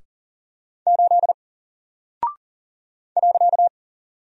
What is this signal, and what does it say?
Morse code, heard as a keyed tone, about 39 wpm, 700 Hz. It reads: 8 .
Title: Single Character (Letter:Number:Punctuation:Prosign) Alt 40wpm